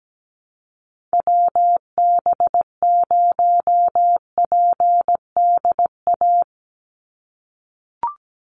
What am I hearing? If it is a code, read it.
Morse code, heard as a keyed tone, about 17 wpm, 700 Hz. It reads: WB0PDA
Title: USA Call Signs 17wpm